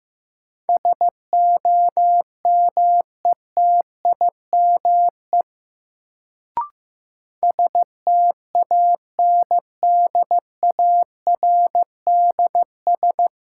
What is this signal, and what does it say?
Morse code, heard as a keyed tone, about 15 wpm, 700 Hz. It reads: SOMETIME STANDARDS
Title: Sentences from Top 1000 Words - Review Part 6 - New Word Drill - 15wpm